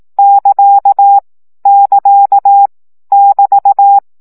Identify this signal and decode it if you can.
Morse code, heard as a keyed tone, about 18 wpm, 800 Hz. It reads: <KA> <KA> <BT>
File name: CW gyakorlas ct 15 baud